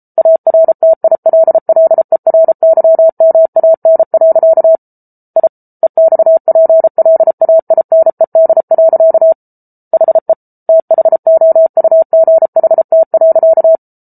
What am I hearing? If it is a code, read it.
Morse code, heard as a keyed tone, about 34 wpm, 650 Hz. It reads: ARTILLERYMAN. I EXPLAINED. HE THOUGHT.